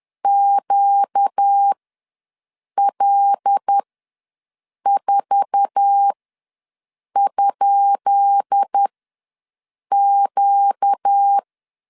Morse code, heard as a keyed tone, about 10 wpm, spaced as for 6 wpm, 800 Hz. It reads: QL4?Q
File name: LEZIONI / 10 SIGNS / 10B INT ws 10wpm 4134